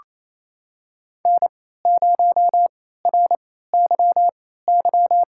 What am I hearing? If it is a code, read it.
Morse code, heard as a keyed tone, about 28 wpm, spaced as for 16 wpm, 700 Hz. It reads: N0RYY
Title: USA Call Signs - 3x Character Spacing - 28wpm